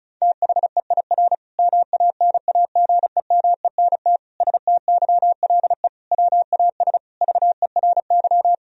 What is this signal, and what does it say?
Morse code, heard as a keyed tone, about 36 wpm, 700 Hz. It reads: THEIR MANAGEMENT STYLE WAS VERY
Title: Sentences from Top 1000 Words - Review Part 4 - 35wpm